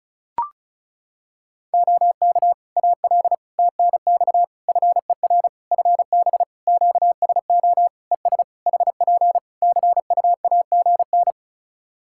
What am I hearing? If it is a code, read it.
Morse code, heard as a keyed tone, about 36 wpm, 700 Hz. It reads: OK AL TNX FER FB QSO ES HP CUAGN
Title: Practice QSO - Exchange 4 - Ending - 35wpm